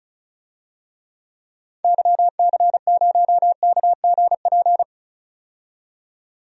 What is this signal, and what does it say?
Morse code, heard as a keyed tone, about 34 wpm, 700 Hz. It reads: YC0KGP
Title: DXCC-100 Call Signs - 6 Characters - 35wpm